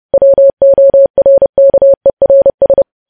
Morse code, heard as a keyed tone, about 30 wpm, 550 Hz. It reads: WORKERS
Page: News Headlines in Morse Code at 30 WPM